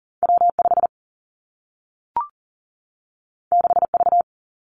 Morse code, heard as a keyed tone, about 40 wpm, 700 Hz. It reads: W5 6V
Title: Two Challenging Letter-Number Combinations 40wpm